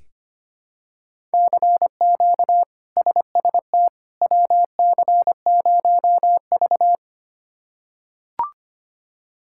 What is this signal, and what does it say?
Morse code, heard as a keyed tone, about 25 wpm, 700 Hz. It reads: CQ SST WC0V
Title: K1USN SST QSO Practice - 25wpm